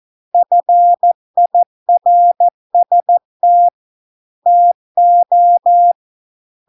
Morse code, heard as a keyed tone, about 14 wpm, 700 Hz. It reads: FIRST TO